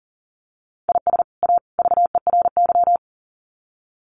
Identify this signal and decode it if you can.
Morse code, heard as a keyed tone, about 38 wpm, 700 Hz. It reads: IS A VERY
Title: Sets of 3 Words 40wpm